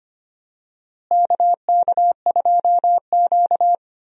Morse code, heard as a keyed tone, about 25 wpm, 700 Hz. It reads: KK2Q
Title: USA Call Signs - 4 Characters - Repeated 3x - 25wpm